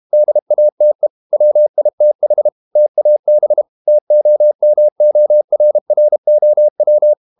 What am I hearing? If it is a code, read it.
Morse code, heard as a keyed tone, about 32 wpm, 600 Hz. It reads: DATE WITH TAB TOMORROW
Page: Never 31wpm | CW med Gnister